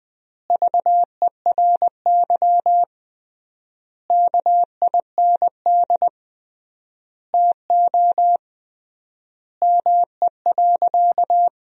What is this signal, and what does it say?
Morse code, heard as a keyed tone, about 21 wpm, 700 Hz. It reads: VERY KIND TO ME.
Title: 3x Word-Spacing - Sentences from Top 200 Words - 20wpm